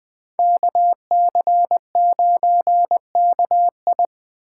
Morse code, heard as a keyed tone, about 20 wpm, 700 Hz. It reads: KC9KI